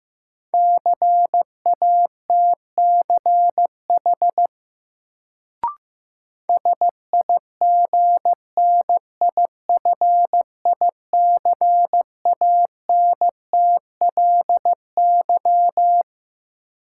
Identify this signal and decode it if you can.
Morse code, heard as a keyed tone, about 15 wpm, 700 Hz. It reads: CATCH SIGNIFICANTLY
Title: Sentences from Top 1500 Words - New Word Drill - 15wpm